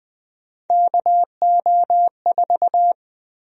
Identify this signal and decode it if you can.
Morse code, heard as a keyed tone, about 20 wpm, 700 Hz. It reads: KO4
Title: USA Call Sign Prefixes - 20wpm